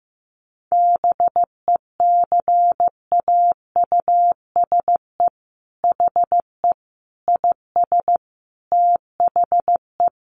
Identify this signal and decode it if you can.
Morse code, heard as a keyed tone, about 15 wpm, 700 Hz. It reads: BECAUSE HE IS THE
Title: Sentences from Top 1500 Words 15wpm